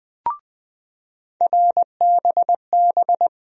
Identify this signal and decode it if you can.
Morse code, heard as a keyed tone, about 21 wpm, 700 Hz. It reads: RBB
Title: USA Call Sign Postfixes - 3 Letters 20wpm